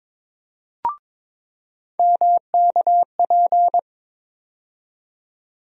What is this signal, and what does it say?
Morse code, heard as a keyed tone, about 22 wpm, 700 Hz. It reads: MKP
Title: USA Call Sign Postfixes - 3 Letters 22wpm